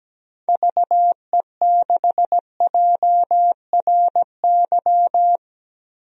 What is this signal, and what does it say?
Morse code, heard as a keyed tone, about 17 wpm, 700 Hz. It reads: VE6JRY
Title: DXCC-100 Call Signs - 6 Characters - Repeated 3x - 17wpm